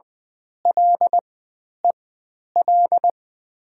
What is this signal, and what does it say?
Morse code, heard as a keyed tone, about 20 wpm, spaced as for 10 wpm, 700 Hz. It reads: LEL
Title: 095 Lesson 16g - Letter c - Call Signs 20wpm